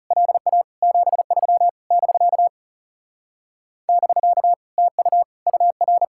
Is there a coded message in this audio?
Yes: Morse code from 0.1 to 6.0 s, heard as a keyed tone, about 40 wpm, 700 Hz. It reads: LA 73 <BK> <BK> TU UR